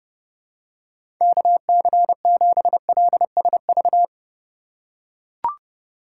Morse code, heard as a keyed tone, about 30 wpm, 700 Hz. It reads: KC7LSV
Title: USA Call Signs - 6 Characters - 4x Character Spacing - 30wpm